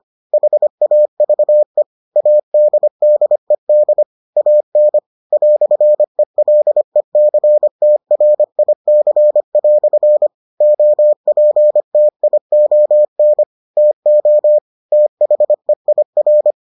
Morse code, heard as a keyed tone, about 25 wpm, 600 Hz. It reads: HAVE ADDED AN "ELECTRIC" OPTION TO THEIR